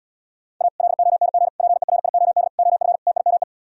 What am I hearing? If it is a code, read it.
Morse code, heard as a keyed tone, about 78 wpm, 700 Hz. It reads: A LONG PERIOD OF TIME